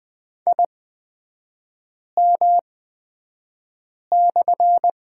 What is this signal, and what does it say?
Morse code, heard as a keyed tone, about 20 wpm, spaced as for 5 wpm, 690 Hz. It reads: IM/
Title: 229 Lesson 37e - Letter x - Three Characters 20wpm